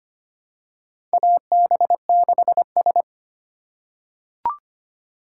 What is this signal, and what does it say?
Morse code, heard as a keyed tone, about 25 wpm, 700 Hz. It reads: AB6S
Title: USA Call Signs - 4 Characters - 4x Character Spacing - 25wpm